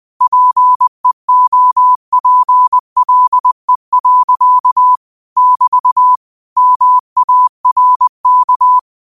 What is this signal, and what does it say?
Morse code, heard as a keyed tone, about 20 wpm, 1000 Hz. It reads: PEOPLE. <BT> MARK